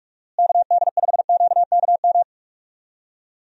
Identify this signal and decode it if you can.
Morse code, heard as a keyed tone, about 46 wpm, 700 Hz. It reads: KD5QKM
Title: USA Call Signs - 6 Characters - Repeated 3x - 45wpm